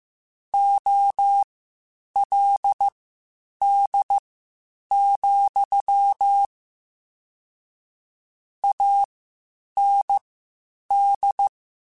Morse code, heard as a keyed tone, about 15 wpm, spaced as for 8 wpm, 800 Hz. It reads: OLD, AND
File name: (spaced) / QSOs Spaced-mp3s / kc7haa @13 sp